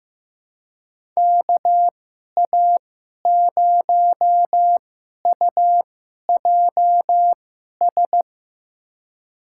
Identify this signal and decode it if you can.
Morse code, heard as a keyed tone, about 15 wpm, spaced as for 11 wpm, 700 Hz. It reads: KA0UJS
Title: USA Call Signs - 6 Characters - 2x Character Spacing - 15wpm